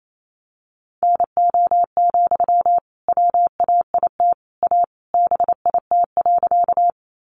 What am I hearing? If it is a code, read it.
Morse code, heard as a keyed tone, about 28 wpm, 700 Hz. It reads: NO, WAIT A BIT.
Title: Sentences from Top 900 Words 28wpm